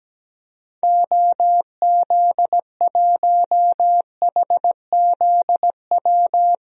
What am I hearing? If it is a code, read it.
Morse code, heard as a keyed tone, about 17 wpm, 700 Hz. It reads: OZ1HZW
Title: DXCC-100 Call Signs - 6 Characters - 17wpm